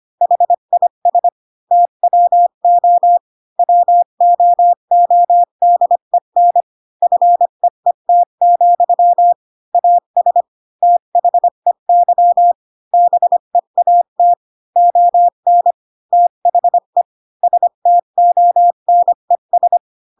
Morse code, heard as a keyed tone, about 25 wpm, 700 Hz. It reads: HIS TWO WOODEN FEET, AS THEY BEAT ON THE STONES
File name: Pinocchio / Pinocchio ch3 20 25 ik1hge